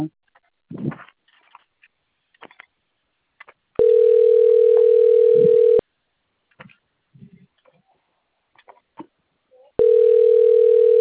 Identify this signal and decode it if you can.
Ringback tone (North America), 440 + 480 Hz.